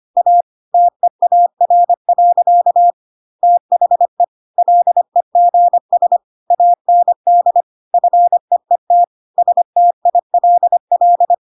Morse code, heard as a keyed tone, about 25 wpm, 700 Hz. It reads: A TEAR. THE LEGS AND FEET STILL
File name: Pinocchio / Pinocchio ch3 25 25 ik1hge